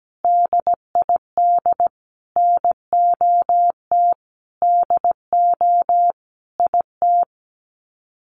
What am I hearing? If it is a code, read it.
Morse code, heard as a keyed tone, about 17 wpm, 700 Hz. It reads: DID NOT DO IT
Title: Sets of 5 Words Encore2 17wpm